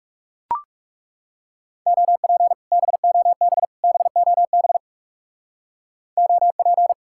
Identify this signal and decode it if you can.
Morse code, heard as a keyed tone, about 36 wpm, 700 Hz. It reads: OP BOB BOB OP
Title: Speed-Racing Common QSO Phrases - 30wpm